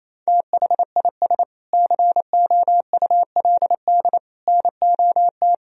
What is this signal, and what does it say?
Morse code, heard as a keyed tone, about 27 wpm, 700 Hz. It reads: THIS COULD NOT
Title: Sentences from Top 1000 Words - Review Part 3 - 28wpm